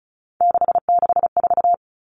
Morse code, heard as a keyed tone, about 34 wpm, 700 Hz. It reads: 664
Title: Common Cut Numbers - 3-Digit 35wpm